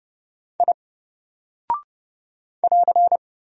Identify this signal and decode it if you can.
Morse code, heard as a keyed tone, about 30 wpm, 700 Hz. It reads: I <AR>
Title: Single Character (Letter:Number:Punctuation:Prosign) 30wpm